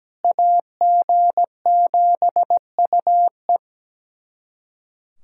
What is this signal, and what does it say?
Morse code, heard as a keyed tone, about 17 wpm, 700 Hz. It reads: AG7UE
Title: USA Callsign DE Callsign - 5 Characters - 17wpm